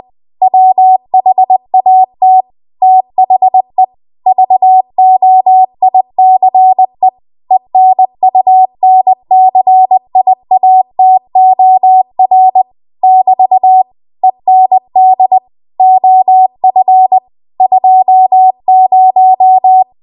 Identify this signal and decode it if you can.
Morse code, heard as a keyed tone, about 20 wpm, 750 Hz. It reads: WHAT THE VOICE ENUNCIATOR <BT> END OF 20
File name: Archive / 20 WPM / 20WPM